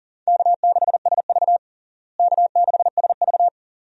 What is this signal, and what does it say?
Morse code, heard as a keyed tone, about 40 wpm, 700 Hz. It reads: K6SV K6SV
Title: USA Call Signs - 4 Characters - Repeated 2x - 40wpm